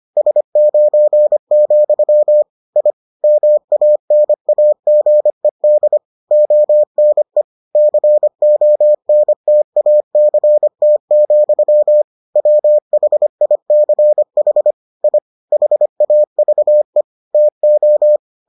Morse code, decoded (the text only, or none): S9, I MANAGED ONE CONTACT, WHICH I HAVE TO